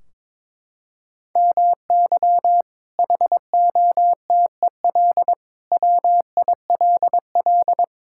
Morse code, heard as a keyed tone, about 22 wpm, 700 Hz. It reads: MY HOTEL WILL
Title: Sentences from Top 1600 Words 22wpm